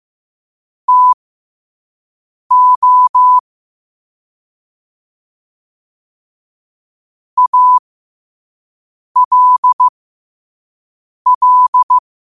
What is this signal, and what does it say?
Morse code, heard as a keyed tone, about 15 wpm, spaced as for 5 wpm, 1000 Hz. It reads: TO ALL
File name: qotdfiles / 5wpm